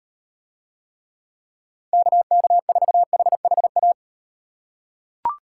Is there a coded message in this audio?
Yes: Morse code from 1.9 to 3.9 s, heard as a keyed tone, about 38 wpm, 700 Hz. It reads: KK4HHA